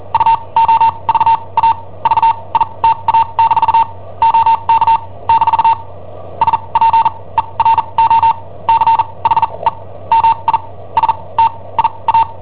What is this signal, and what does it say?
Morse code, heard as a keyed tone, about 43 wpm, spaced as for 27 wpm, 910 Hz. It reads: UOVA VITA- OK - SPERO CHE MI STIA
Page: Cassetta esercizi preparata da I7OHP per IW5ELP